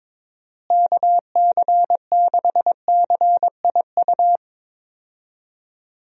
Morse code, heard as a keyed tone, about 22 wpm, 700 Hz. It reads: KC6CIU
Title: USA Call Signs - 6 Characters - Repeated 3x - 22wpm